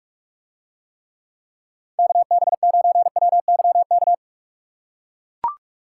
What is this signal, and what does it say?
Morse code, heard as a keyed tone, about 44 wpm, 700 Hz. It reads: KB9WYK